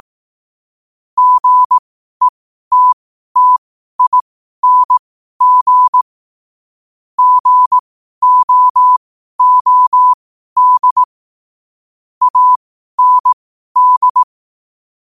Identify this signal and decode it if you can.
Morse code, heard as a keyed tone, about 18 wpm, spaced as for 12 wpm, 1000 Hz. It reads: GETTING GOOD AND
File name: qotdfiles / 12wpm